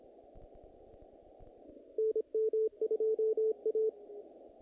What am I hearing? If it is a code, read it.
Morse code, heard as a keyed tone, about 25 wpm, 440 Hz. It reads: NM2A
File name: contests / naqp-cw-k5tr-op-k5pi- -2000